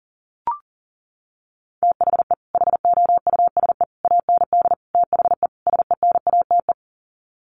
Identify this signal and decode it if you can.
Morse code, heard as a keyed tone, about 40 wpm, 700 Hz. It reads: THE HOUSE AND THE SENATE
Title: Sets of 5 Words Encore2 40wpm